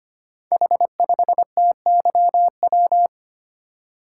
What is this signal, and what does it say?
Morse code, heard as a keyed tone, about 25 wpm, 700 Hz. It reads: H5TYW
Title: Code Groups - 5 Letter:Numbers - 25wpm